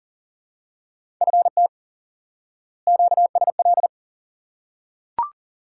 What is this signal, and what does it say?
Morse code, heard as a keyed tone, about 40 wpm, 700 Hz. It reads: FT QSL
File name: IQE Course 3 - Lesson 89 - YAGI - Two Elements - 40wpm